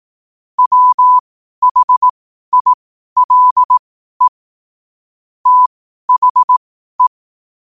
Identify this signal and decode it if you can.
Morse code, heard as a keyed tone, about 18 wpm, spaced as for 12 wpm, 1000 Hz. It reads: WHILE THE
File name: qotdfiles / 12wpm